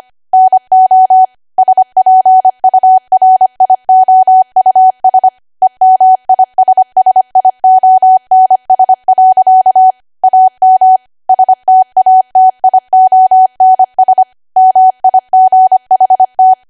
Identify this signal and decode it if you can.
Morse code, heard as a keyed tone, about 25 wpm, 750 Hz. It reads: NO SPURIOUS EMISSIONS. AM STATIONS MIGHT
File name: Archive / 25 WPM / 25WPM